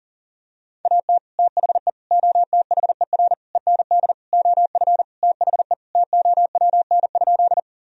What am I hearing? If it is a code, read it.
Morse code, heard as a keyed tone, about 37 wpm, 700 Hz. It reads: AT THE OTHER END OF THE TOWN?
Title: Sentences from Top 1000 Words - Review Part 2 - 40wpm